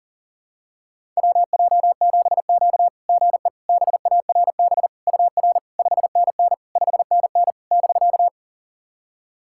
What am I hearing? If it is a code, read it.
Morse code, heard as a keyed tone, about 44 wpm, 700 Hz. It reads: WJ7Q GE BARB UR 5NN 5NN <BK>
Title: SOTA QSO Practice 40wpm